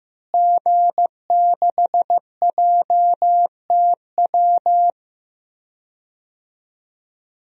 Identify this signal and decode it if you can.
Morse code, heard as a keyed tone, about 15 wpm, 700 Hz. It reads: G6JTW
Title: DXCC-100 Call Signs - 5 Characters - Repeated 3x - 15wpm